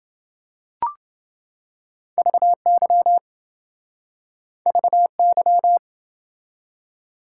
Morse code, heard as a keyed tone, about 28 wpm, 700 Hz. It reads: VY VY